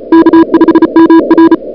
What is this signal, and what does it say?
Morse code, heard as a keyed tone, about 34 wpm, 330 Hz. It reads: K5MR
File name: CWSprint / February 2007 CW Sprint / 01-2007 CW Sprint Hour1